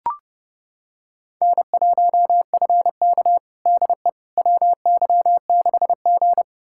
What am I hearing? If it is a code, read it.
Morse code, heard as a keyed tone, about 32 wpm, 700 Hz. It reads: N1FK DE WY6G
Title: USA Callsign DE Callsign - 4 Characters - 30wpm